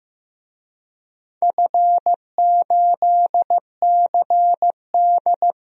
Morse code, heard as a keyed tone, about 15 wpm, 700 Hz. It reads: F8CD
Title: DXCC-100 Call Signs - 4 Characters - 15wpm